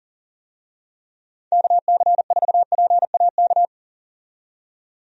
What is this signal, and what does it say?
Morse code, heard as a keyed tone, about 40 wpm, 700 Hz. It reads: KC4PAK
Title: USA Call Signs - 6 Characters - 40wpm